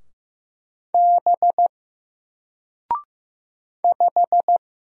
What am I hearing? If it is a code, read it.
Morse code, heard as a keyed tone, about 15 wpm, 700 Hz. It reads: B 5
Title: Single Letter-Number 15wpm